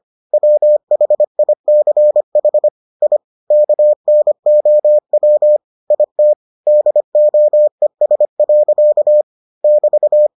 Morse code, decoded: WHICH I KNOW IT DOES. <BT>

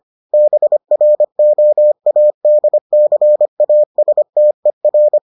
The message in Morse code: BROADCASTER

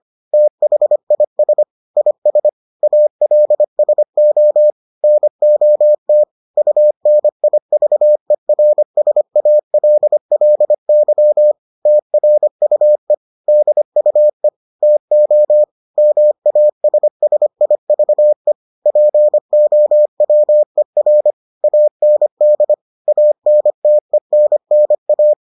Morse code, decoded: THIS IS ALSO NOT UNIVERSALLY TRUE DUE TO MASSIVE POWER AND ANTENNA

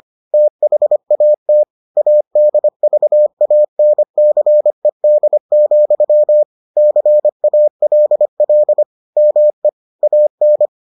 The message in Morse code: THAT ADVANCED, CALL ME AN

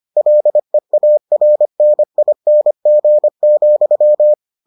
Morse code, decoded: LEARNING,

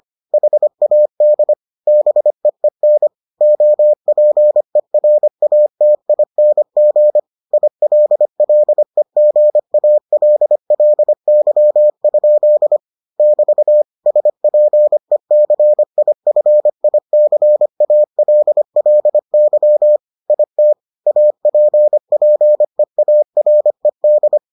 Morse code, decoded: HAD BEEN OPERATING ILLEGALLY? <BT> SPECIFICALLY IT APPEARED

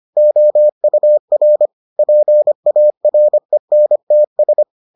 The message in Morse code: OUR PARENTS

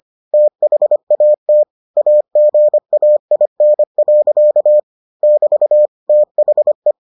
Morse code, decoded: THAT AGAIN. <BT> THE

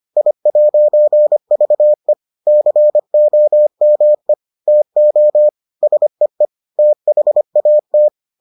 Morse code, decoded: I'VE COME TO SEE THAT

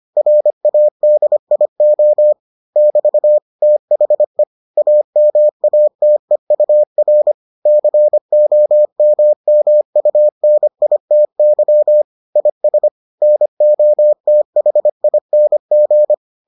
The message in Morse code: RADIO <BT> THE AMATEUR COMMUNITY IS NOTHING

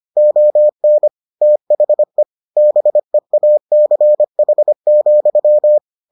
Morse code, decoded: ON THE BEACH,

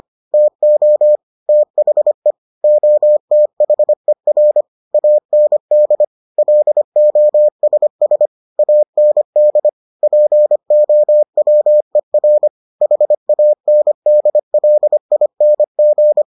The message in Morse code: TO THE OTHER AND LOSS AND POWER HANDLING

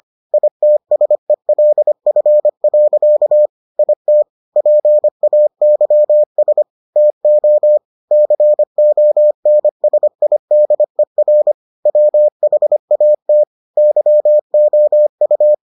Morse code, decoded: ITSELF. IT PAYS TO CONSIDER WHAT YOU

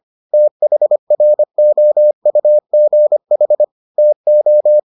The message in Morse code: THROUGH TO

